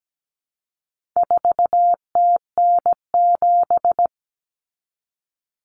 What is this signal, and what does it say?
Morse code, heard as a keyed tone, about 17 wpm, 700 Hz. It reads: 4TN7
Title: Common Cut Numbers - 4-Digit 17wpm